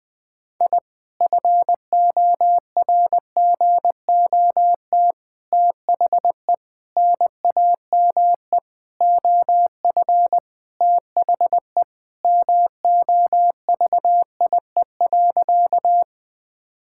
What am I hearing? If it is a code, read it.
Morse code, heard as a keyed tone, about 21 wpm, 690 Hz. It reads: I FORGOT THE NAME OF THE MOVIE.
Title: Sentences with 4-Word Phrases - Spoken Beginning - 20wpm